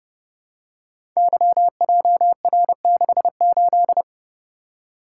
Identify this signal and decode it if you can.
Morse code, heard as a keyed tone, about 28 wpm, 700 Hz. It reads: YJR68
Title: Code Groups - 5 Letter:Numbers - 30wpm